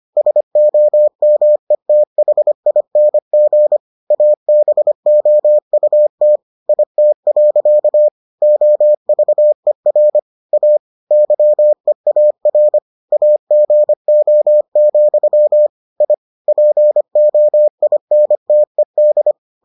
Morse code, heard as a keyed tone, about 25 wpm, 600 Hz. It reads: SOMETHING ABOUT IT. OVER A YEAR AGO, I POINTED